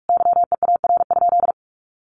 Morse code, heard as a keyed tone, about 44 wpm, 700 Hz. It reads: YEAR?
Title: Sentences from Top 1500 Words 45wpm